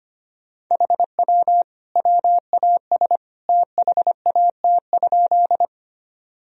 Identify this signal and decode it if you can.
Morse code, heard as a keyed tone, about 25 wpm, 700 Hz. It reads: HW WAS THAT?
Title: Sentences from Top 100 Words and Abbreviations 25wpm